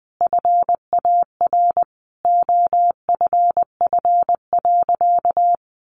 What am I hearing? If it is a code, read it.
Morse code, heard as a keyed tone, about 20 wpm, 700 Hz. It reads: FAR OFF.
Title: Sentences from Top 1400 Words 20wpm